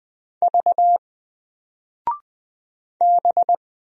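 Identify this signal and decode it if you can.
Morse code, heard as a keyed tone, about 20 wpm, 700 Hz. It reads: V B
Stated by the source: Single letter and number Morse drill at 20 wpm, from N4LCD.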